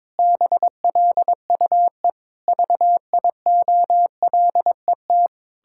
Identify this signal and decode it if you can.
Morse code, heard as a keyed tone, about 22 wpm, 700 Hz. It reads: BLUE VIOLET
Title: Pangram Challenge v2 - 22wpm